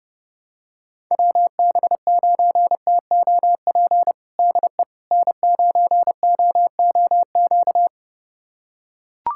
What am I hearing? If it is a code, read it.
Morse code, heard as a keyed tone, about 30 wpm, 700 Hz. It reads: WB9TOP DE N9OOQ
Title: USA Callsign DE Callsign 30wpm